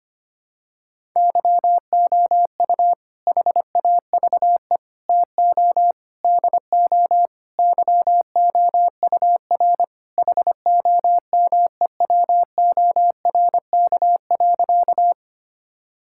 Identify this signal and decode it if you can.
Morse code, heard as a keyed tone, about 25 wpm, 700 Hz. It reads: YOU HAVE TO DO YOUR HOMEWORK.
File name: practice / 25wpm / Sentences with 4-Word Phrases - 25wpm